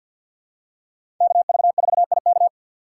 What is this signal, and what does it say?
Morse code, heard as a keyed tone, about 49 wpm, 700 Hz. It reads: KV4IK
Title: USA Call Signs - 5 Characters - 6x Character Spacing - 50wpm